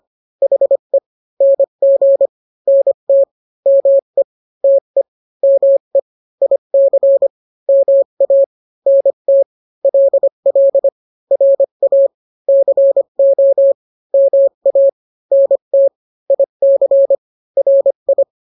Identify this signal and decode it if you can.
Morse code, heard as a keyed tone, about 25 wpm, 550 Hz. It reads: HE NG NT ME TE ME IC MA NT LL RA CO MA NT IC RI